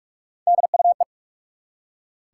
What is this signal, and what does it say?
Morse code, heard as a keyed tone, about 43 wpm, 700 Hz. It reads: DUE
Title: Three Letter Words - 45wpm